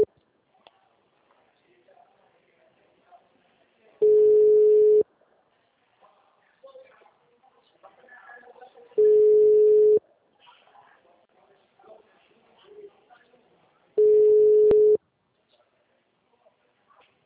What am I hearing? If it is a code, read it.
Ringback tone (Europe, 425 Hz), with speech.